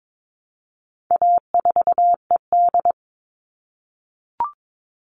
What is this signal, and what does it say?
Morse code, heard as a keyed tone, about 22 wpm, 700 Hz. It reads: A4ED
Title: All Cut Numbers - 4-Digit 22wpm